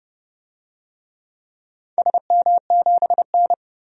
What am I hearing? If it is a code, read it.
Morse code, heard as a keyed tone, about 32 wpm, 700 Hz. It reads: SM7N
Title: DXCC-100 Call Signs - 4 Characters - Repeated 2x - 30wpm